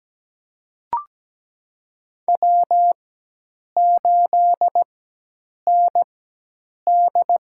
Morse code, heard as a keyed tone, about 17 wpm, spaced as for 8 wpm, 700 Hz. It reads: W8ND
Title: USA Call Signs - 4x Character Spacing - 17wpm